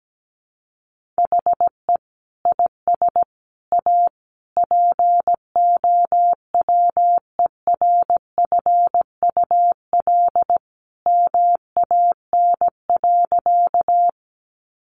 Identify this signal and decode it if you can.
Morse code, heard as a keyed tone, about 17 wpm, 700 Hz. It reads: HE IS A POWERFUL MAN.
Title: Sentences from Top 1300 Words 17wpm